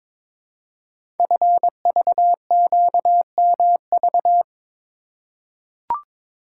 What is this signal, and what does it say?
Morse code, heard as a keyed tone, about 22 wpm, 700 Hz. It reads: FVQMV
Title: Code Groups - 5 Letter:Numbers - 22wpm